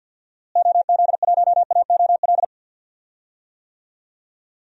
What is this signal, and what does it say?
Morse code, heard as a keyed tone, about 52 wpm, 700 Hz. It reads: OZ1AOL
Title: DXCC-100 Call Signs - 6 Characters - Repeated 2x - 50wpm